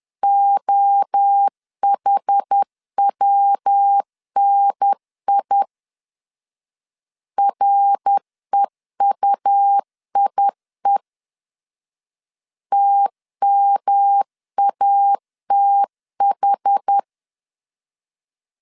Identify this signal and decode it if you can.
Morse code, heard as a keyed tone, about 10 wpm, 800 Hz. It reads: OHWNI REUIE TMATH
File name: CW / DET / 1 3 1500C S B 10wpm 3335